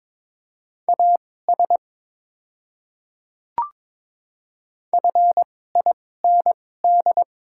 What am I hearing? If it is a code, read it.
Morse code, heard as a keyed tone, about 22 wpm, spaced as for 16 wpm, 690 Hz. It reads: AS FIND
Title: 2x Character Spacing - Top 100 Words - 22wpm